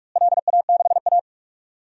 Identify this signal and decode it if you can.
Morse code, heard as a keyed tone, about 46 wpm, 700 Hz. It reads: LA6A
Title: DXCC-100 Call Signs - 4 Characters - Repeated 3x - 45wpm